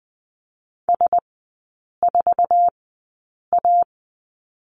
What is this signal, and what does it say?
Morse code, heard as a keyed tone, about 20 wpm, spaced as for 8 wpm, 700 Hz. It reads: S4A
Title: 047 Lesson 9e - Number 4 - Three Characters 20wpm